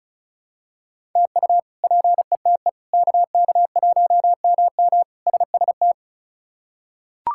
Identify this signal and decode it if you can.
Morse code, heard as a keyed tone, about 36 wpm, 700 Hz. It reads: TU PETE KK1MM SST
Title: K1USN SST QSO Practice - 35wpm